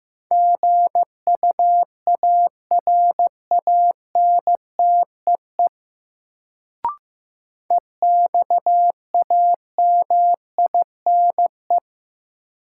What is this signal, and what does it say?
Morse code, heard as a keyed tone, about 15 wpm, 700 Hz. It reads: GUARANTEE EXAMINE
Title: Sentences from Top 1600 Words - New Word Drill Alt - 15wpm